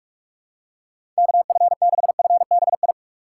Morse code, heard as a keyed tone, about 44 wpm, 700 Hz. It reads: KF6FBI
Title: USA Call Signs - 6 Characters - 3x Character Spacing - 45wpm